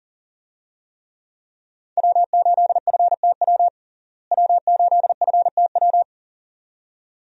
Morse code, heard as a keyed tone, about 40 wpm, 700 Hz. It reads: W8FTW W8FTW
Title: USA Call Signs - 5 Characters - Repeated 2x - 40wpm